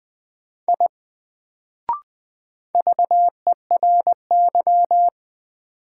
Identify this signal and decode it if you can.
Morse code, heard as a keyed tone, about 21 wpm, 700 Hz. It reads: I VERY
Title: Sentences from Top 200 Words - New Word Drill - 20wpm